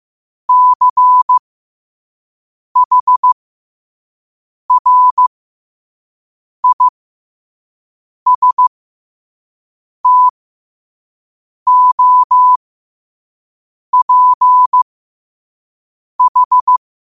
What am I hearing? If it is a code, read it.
Morse code, heard as a keyed tone, about 15 wpm, spaced as for 5 wpm, 1000 Hz. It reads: CHRISTOPH
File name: qotdfiles / 5wpm